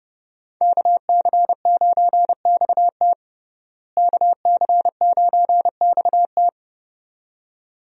Morse code, heard as a keyed tone, about 30 wpm, 700 Hz. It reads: KC9XT KC9XT
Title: USA Call Signs - 5 Characters - Repeated 2x - 30wpm